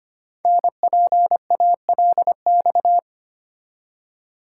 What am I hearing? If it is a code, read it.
Morse code, heard as a keyed tone, about 25 wpm, 700 Hz. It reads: NPALX
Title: Code Groups - 5 Letter:Numbers - 25wpm